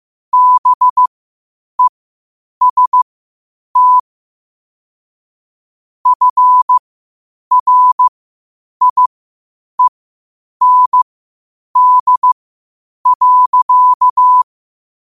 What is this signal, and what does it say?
Morse code, heard as a keyed tone, about 15 wpm, spaced as for 8 wpm, 1000 Hz. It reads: BEST FRIEND.
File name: qotdfiles / 8wpm